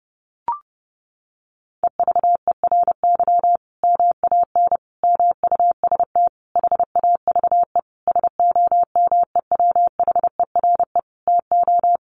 Morse code, heard as a keyed tone, about 30 wpm, 700 Hz. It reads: EVERY MAN MUST HAVE SOMEWHERE TO
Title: Sentences from Top 1400 Words 30wpm